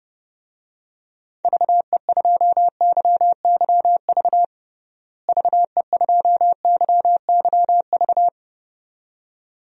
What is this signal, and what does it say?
Morse code, heard as a keyed tone, about 30 wpm, 700 Hz. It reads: VE2YYV VE2YYV
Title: DXCC-100 Call Signs - 6 Characters - Repeated 2x - 30wpm